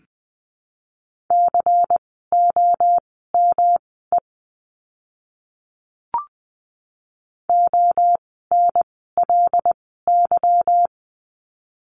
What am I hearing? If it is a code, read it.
Morse code, heard as a keyed tone, about 20 wpm, spaced as for 15 wpm, 700 Hz. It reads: COME ONLY
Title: 237 Lesson 38f - Letter q - Words 20wpm